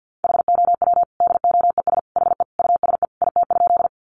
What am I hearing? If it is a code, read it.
Morse code, heard as a keyed tone, about 50 wpm, 700 Hz. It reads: HOW DOES HE USE IT?
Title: Sentences from Top 200 Words 50wpm